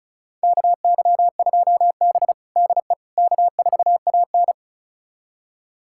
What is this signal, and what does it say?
Morse code, heard as a keyed tone, about 36 wpm, 700 Hz. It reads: KY2B DE K4AN
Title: USA Callsign DE Callsign - 4 Characters - 35wpm